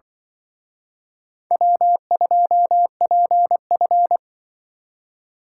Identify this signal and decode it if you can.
Morse code, heard as a keyed tone, about 23 wpm, 700 Hz. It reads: W2PF
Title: Speed Racing - USA Call Signs 20wpm